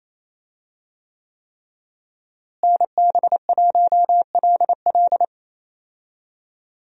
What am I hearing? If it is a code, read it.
Morse code, heard as a keyed tone, about 28 wpm, 700 Hz. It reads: NB1LL
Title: USA Call Signs - 5 Characters - Repeated 3x - 28wpm